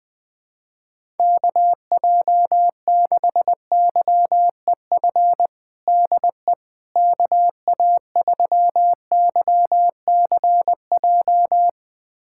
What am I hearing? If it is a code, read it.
Morse code, heard as a keyed tone, about 20 wpm, 700 Hz. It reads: KJ6YEF DE KA3YCJ
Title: USA Callsign DE Callsign 20wpm